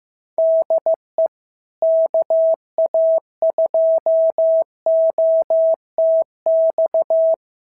Morse code, heard as a keyed tone, about 14 wpm, 650 Hz. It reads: DE KA2OTX
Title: SOTA QSO Practice 15wpm